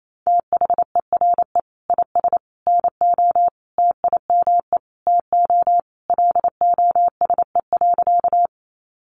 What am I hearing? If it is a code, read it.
Morse code, heard as a keyed tone, about 28 wpm, 700 Hz. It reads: THERE IS NO TIME TO LOSE.
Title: Sentences from Top 1100 Words 28wpm